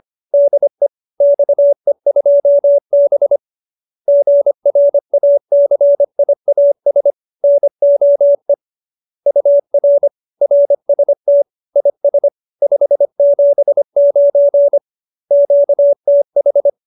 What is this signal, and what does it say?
Morse code, heard as a keyed tone, about 25 wpm, 570 Hz. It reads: DE XE2B GRACIAS NOE UR RST IS 579 QTH
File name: QSO304 25